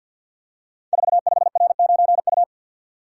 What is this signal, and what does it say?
Morse code, heard as a keyed tone, about 51 wpm, 690 Hz. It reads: 45R9U
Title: Code Groups - 5 Letter:Numbers - 50wpm